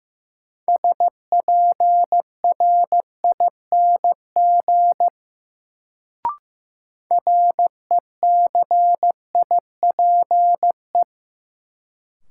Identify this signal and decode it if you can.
Morse code, heard as a keyed tone, about 15 wpm, 700 Hz. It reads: SPRING RECIPE
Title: Sentences from Top 1200 Words - New Word Drill - 15wpm